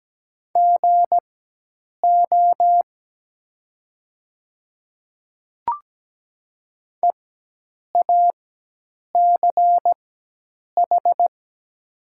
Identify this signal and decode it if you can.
Morse code, heard as a keyed tone, about 17 wpm, spaced as for 8 wpm, 700 Hz. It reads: GO EACH
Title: 4x Character Spacing - Top 100 Words - 17wpm